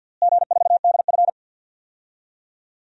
Morse code, heard as a keyed tone, about 52 wpm, 700 Hz. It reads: G4DF
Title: DXCC-100 Call Signs - 4 Characters - Repeated 2x - 50wpm